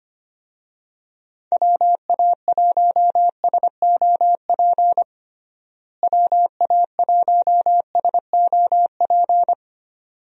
Morse code, heard as a keyed tone, about 25 wpm, 700 Hz. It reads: WA1SOP WA1SOP
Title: USA Call Signs - 6 Characters - Repeated 2x - 25wpm